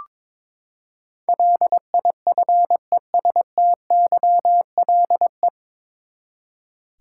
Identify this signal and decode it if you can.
Morse code, heard as a keyed tone, about 22 wpm, 700 Hz. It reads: LIFESTYLE
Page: Morse Code Ninja practice 22wpm